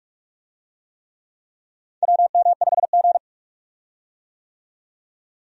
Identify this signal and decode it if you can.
Morse code, heard as a keyed tone, about 46 wpm, 700 Hz. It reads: WM5G
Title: USA Call Signs - 4 Characters - 45wpm